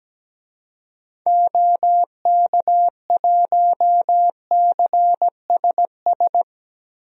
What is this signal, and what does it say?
Morse code, heard as a keyed tone, about 17 wpm, 700 Hz. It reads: OK1CSS
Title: DXCC-100 Call Signs - 6 Characters - 17wpm